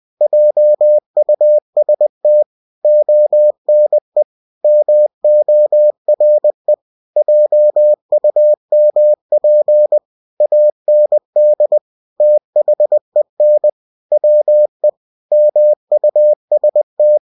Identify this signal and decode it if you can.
Morse code, heard as a keyed tone, about 20 wpm, 600 Hz. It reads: JUST ONE MORE JUMP AND THEN WE MUST